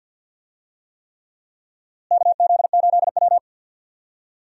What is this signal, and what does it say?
Morse code, heard as a keyed tone, about 47 wpm, 700 Hz. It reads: KZ8W
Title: USA Call Signs - 4 Characters - Repeated 3x - 50wpm